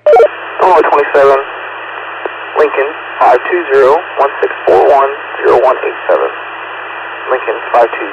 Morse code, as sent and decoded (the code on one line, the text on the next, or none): none